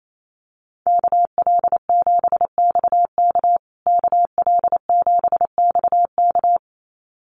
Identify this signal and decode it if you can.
Morse code, heard as a keyed tone, about 28 wpm, 700 Hz. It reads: KL7XK KL7XK
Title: USA Call Signs Repeated 2x 28wpm